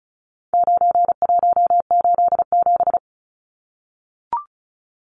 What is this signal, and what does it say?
Morse code, heard as a keyed tone, about 34 wpm, 700 Hz. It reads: 9187
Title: Numbers - 4-Digit 35wpm